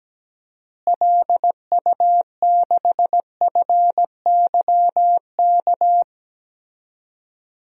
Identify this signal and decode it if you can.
Morse code, heard as a keyed tone, about 17 wpm, 700 Hz. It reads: LU6FYK
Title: DXCC-100 Call Signs - 6 Characters - Repeated 3x - 17wpm